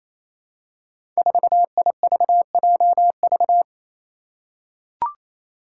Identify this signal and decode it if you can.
Morse code, heard as a keyed tone, about 27 wpm, 700 Hz. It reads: 4IVJV
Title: Five Challenging Letter-Number Combinations 28wpm